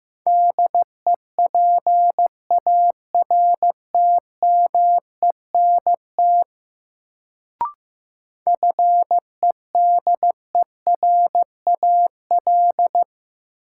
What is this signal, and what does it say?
Morse code, heard as a keyed tone, about 15 wpm, 700 Hz. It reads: DEPARTMENT FEDERAL
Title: Sentences from Top 1000 Words - Review Part 5 - New Word Drill - 15wpm